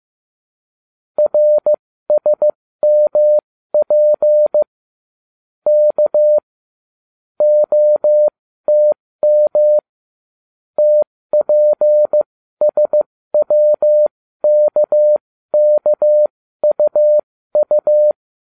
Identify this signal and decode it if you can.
Morse code, heard as a keyed tone, about 15 wpm, spaced as for 13 wpm, 600 Hz. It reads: RSMP K OTM TPSWKKUU